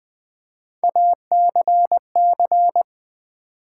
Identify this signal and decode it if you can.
Morse code, heard as a keyed tone, about 20 wpm, 700 Hz. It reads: ACC
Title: Build A Word - 3 Letters At A Time - 8-Letter Words - 20wpm